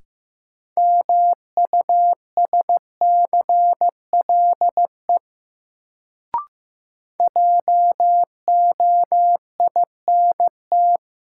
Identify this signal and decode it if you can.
Morse code, heard as a keyed tone, about 15 wpm, 700 Hz. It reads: MUSCLE JOINT
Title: Sentences from Top 1300 Words - New Word Drill - 15wpm